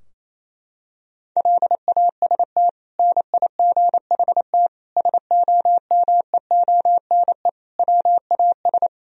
Morse code, heard as a keyed tone, about 28 wpm, 700 Hz. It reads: LAST NIGHT SOMEONE WAS
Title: Sentences from Top 1000 Words - Review Part 4 - 28wpm